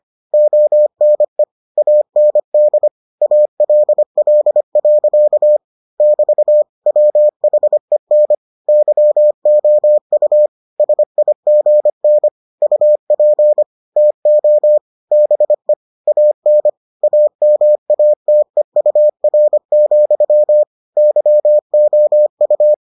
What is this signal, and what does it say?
Morse code, heard as a keyed tone, about 25 wpm, 600 Hz. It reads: ONE AND ALL. <BT> WHEN YOU SIGN UP TO BE AN AMATEUR, YOU